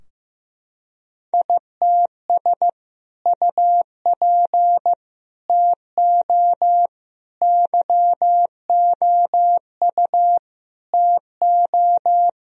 Morse code, heard as a keyed tone, about 16 wpm, 700 Hz. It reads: ITS UP TO YOU TO
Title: Sentences from Top 1000 Words - Review Part 2 - 15wpm